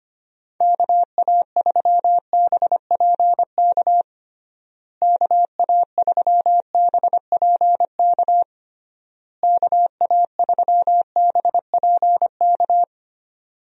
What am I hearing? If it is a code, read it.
Morse code, heard as a keyed tone, about 25 wpm, 700 Hz. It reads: KA3BPK KA3BPK KA3BPK
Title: USA Call Signs - 6 Characters - Repeated 3x - 25wpm